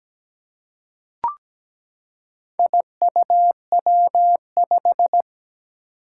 Morse code, heard as a keyed tone, about 17 wpm, 700 Hz. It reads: IUW5
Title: Four Challenging Letter-Number Combinations 17wpm